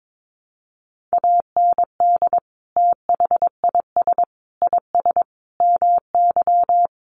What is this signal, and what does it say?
Morse code, heard as a keyed tone, about 22 wpm, 700 Hz. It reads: AND THIS IS MY
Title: Sentences from Top 1200 Words 22wpm